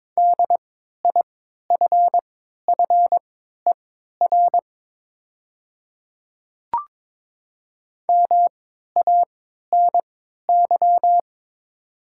Morse code, heard as a keyed tone, about 22 wpm, spaced as for 12 wpm, 700 Hz. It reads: DIFFER MANY
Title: 3x Character Spacing - Top 300 Words - 22wpm